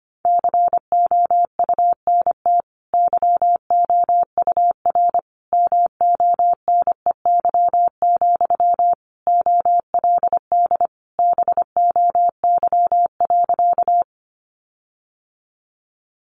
Morse code, heard as a keyed tone, about 25 wpm, 700 Hz. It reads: COUNT YOUR MONEY, OLD BOY.
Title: Sentences from Top 1500 Words 25wpm